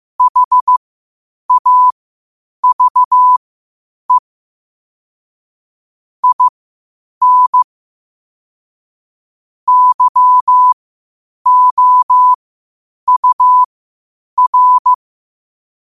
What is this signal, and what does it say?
Morse code, heard as a keyed tone, about 15 wpm, spaced as for 8 wpm, 1000 Hz. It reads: HAVE IN YOUR
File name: qotdfiles / 8wpm